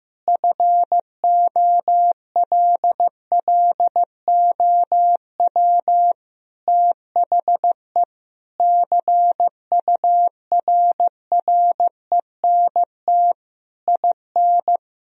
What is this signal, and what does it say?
Morse code, heard as a keyed tone, about 15 wpm, 700 Hz. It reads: FOLLOW THE CURRENT IN
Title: Sentences from Top 1000 Words - Review Part 4 - 15wpm